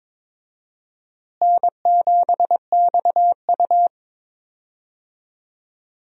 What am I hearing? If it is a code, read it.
Morse code, heard as a keyed tone, about 22 wpm, 700 Hz. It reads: N7XU
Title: USA Call Signs - 4 Characters - Repeated 3x - 22wpm